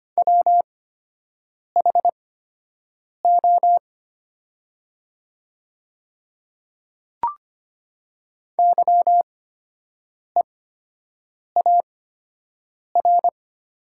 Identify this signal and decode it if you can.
Morse code, heard as a keyed tone, about 25 wpm, spaced as for 7 wpm, 700 Hz. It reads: WHO YEAR
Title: 8x Character Spacing - Top 300 Words - 25wpm